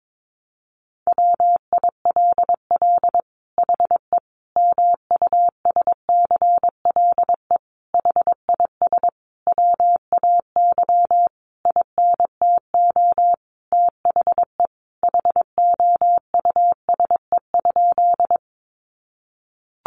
Morse code, heard as a keyed tone, about 22 wpm, 700 Hz. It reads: WILL HE MUSCLE HIS WAY INTO THE HOUSE?
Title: Sentences from Top 1300 Words 22wpm